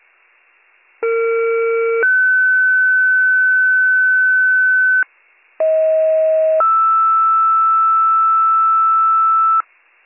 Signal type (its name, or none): two-tone page